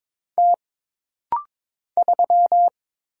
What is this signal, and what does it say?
Morse code, heard as a keyed tone, about 22 wpm, 700 Hz. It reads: T 3
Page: This rapid-fire timing is geared to push your ICR (Instant Character Recognition) skills.